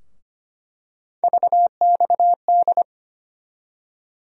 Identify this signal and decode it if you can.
Morse code, heard as a keyed tone, about 25 wpm, 700 Hz. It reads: VXD
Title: USA Call Sign Postfixes - 3 Letters 25wpm